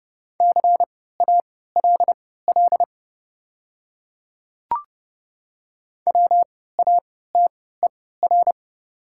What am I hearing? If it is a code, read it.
Morse code, heard as a keyed tone, about 30 wpm, spaced as for 17 wpm, 700 Hz. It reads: CALL WATER
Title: 3x Character Spacing - Top 100 Words - 30wpm